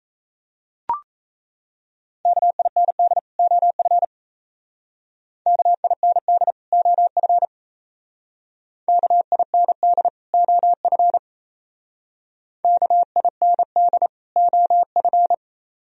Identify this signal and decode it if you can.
Morse code, heard as a keyed tone, about 37 wpm, 700 Hz. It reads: KIND OF KIND OF KIND OF KIND OF